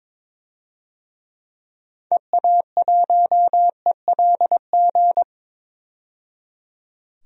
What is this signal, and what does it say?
Morse code, heard as a keyed tone, about 21 wpm, 700 Hz. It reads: EA1ELG
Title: DXCC-100 Call Signs - 6 Characters - 22wpm